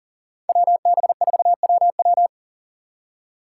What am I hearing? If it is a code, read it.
Morse code, heard as a keyed tone, about 48 wpm, 700 Hz. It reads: WB4WW
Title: Speed Racing - USA 2x2 Call Signs 40wpm